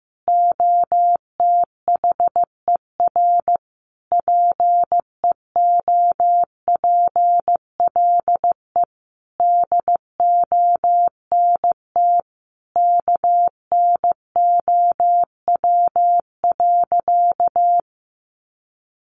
Morse code, heard as a keyed tone, about 15 wpm, 700 Hz. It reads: OTHER PEOPLE DONT KNOW.
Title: Sentences from Top 600 Words 15wpm